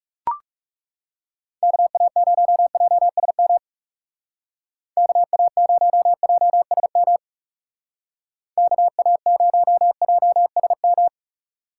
Morse code, heard as a keyed tone, about 39 wpm, 700 Hz. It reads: KA0JSM KA0JSM KA0JSM